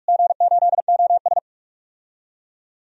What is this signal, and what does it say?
Morse code, heard as a keyed tone, about 47 wpm, 700 Hz. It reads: G8OS
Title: DXCC-100 Call Signs - 4 Characters - Repeated 2x - 45wpm